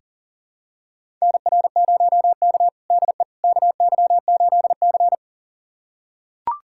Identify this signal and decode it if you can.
Morse code, heard as a keyed tone, about 40 wpm, 700 Hz. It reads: NR0K DE KY8C
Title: USA Callsign DE Callsign - 4 Characters - 40wpm